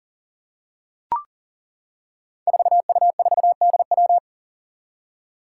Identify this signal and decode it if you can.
Morse code, heard as a keyed tone, about 37 wpm, 700 Hz. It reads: 4U4DW
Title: Five Challenging Letter-Number Combinations 40wpm